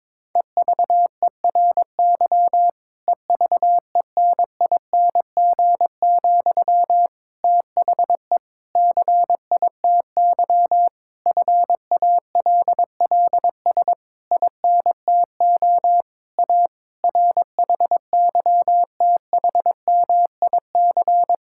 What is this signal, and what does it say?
Morse code, heard as a keyed tone, about 22 wpm, 700 Hz. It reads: EVERY EVENING, THE CITY FALLS INTO A RHYTHMIC
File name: Difficult Sentences with Hints - 22wpm